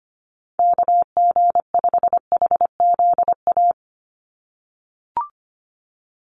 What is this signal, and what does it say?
Morse code, heard as a keyed tone, about 25 wpm, 700 Hz. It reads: KG5HZA
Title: USA Call Signs 25wpm